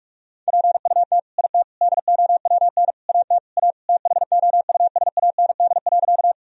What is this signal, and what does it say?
Morse code, heard as a keyed tone, about 42 wpm, 700 Hz. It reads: PUT IT DOWN AT A THOUSAND.
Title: Sentences from Top 1000 Words - Review Part 3 - 45wpm